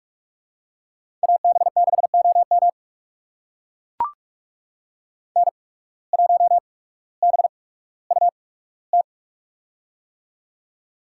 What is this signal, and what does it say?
Morse code, heard as a keyed tone, about 42 wpm, spaced as for 19 wpm, 700 Hz. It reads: AB6OM N1BUT
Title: USA Call Signs - 5 Characters - 8x Character Spacing - 45wpm